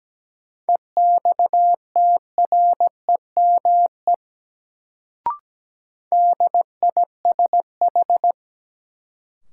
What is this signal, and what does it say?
Morse code, heard as a keyed tone, about 17 wpm, 700 Hz. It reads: EXTREME DISH
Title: Sentences from Top 1300 Words - New Word Drill - 17wpm